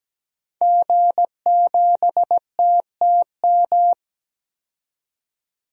Morse code, heard as a keyed tone, about 17 wpm, 700 Hz. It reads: G7TTM
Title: DXCC-100 Call Signs - 5 Characters - 17wpm